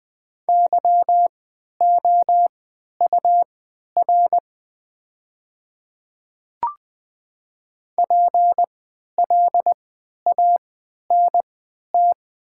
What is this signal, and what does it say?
Morse code, heard as a keyed tone, about 20 wpm, spaced as for 11 wpm, 700 Hz. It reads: YOUR PLANT
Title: 3x Character Spacing - Top 500 Words - 20wpm